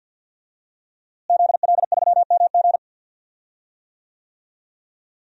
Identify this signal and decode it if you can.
Morse code, heard as a keyed tone, about 50 wpm, 700 Hz. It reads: ZL3MG
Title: DXCC-100 Call Signs - 5 Characters - Repeated 2x - 50wpm